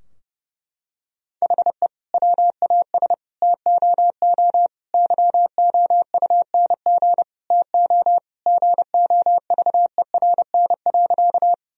Morse code, heard as a keyed tone, about 30 wpm, 700 Hz. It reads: HE WAS TOO YOUNG TO GOVERN.
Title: Sentences from Top 1000 Words - Review Part 2 - 30wpm